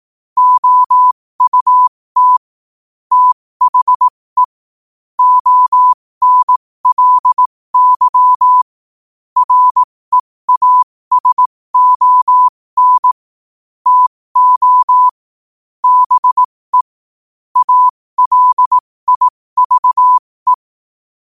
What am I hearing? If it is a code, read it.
Morse code, heard as a keyed tone, about 18 wpm, spaced as for 15 wpm, 1000 Hz. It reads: OUT THE ONLY REASON TO BE ALIVE